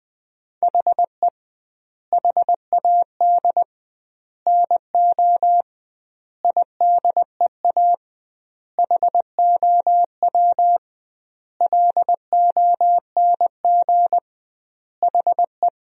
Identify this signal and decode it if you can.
Morse code, heard as a keyed tone, about 20 wpm, 700 Hz. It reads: HE HAD NO IDEA HOW LONG HE
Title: 2x Word-Spacing - Sentences from Top 300 Words - 20wpm